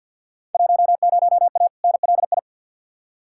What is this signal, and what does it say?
Morse code, heard as a keyed tone, about 52 wpm, 700 Hz. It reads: 10A NLI
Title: Field Day Exchanges Encore - 50wpm